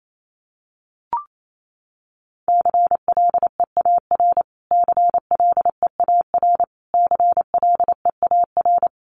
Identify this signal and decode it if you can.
Morse code, heard as a keyed tone, about 28 wpm, 700 Hz. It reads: CLEAR CLEAR CLEAR
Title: Top 500 Words Repeated 28wpm